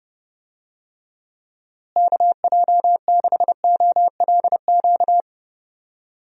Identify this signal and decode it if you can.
Morse code, heard as a keyed tone, about 30 wpm, 700 Hz. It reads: KJ6OLQ
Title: USA Call Signs - 6 Characters - Repeated 2x - 30wpm